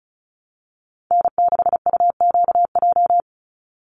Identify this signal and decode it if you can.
Morse code, heard as a keyed tone, about 34 wpm, 700 Hz. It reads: N6UQJ
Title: USA Call Signs 35wpm